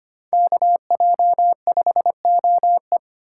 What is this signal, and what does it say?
Morse code, heard as a keyed tone, about 33 wpm, 700 Hz. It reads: KJ5OE